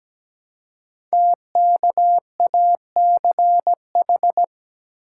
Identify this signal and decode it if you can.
Morse code, heard as a keyed tone, about 17 wpm, 700 Hz. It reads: TKACH
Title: Code Groups - 5 Letters - 17wpm